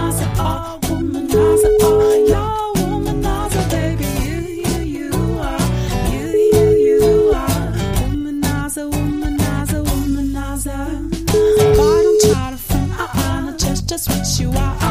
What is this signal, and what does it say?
Ringback tone (Europe, 425 Hz), with noise.